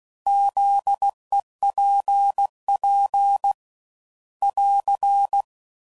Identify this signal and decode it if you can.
Morse code, heard as a keyed tone, about 16 wpm, 800 Hz. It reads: ZEPP <AR>
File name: CW Code Groups @15WPM / fromSM's / Antennas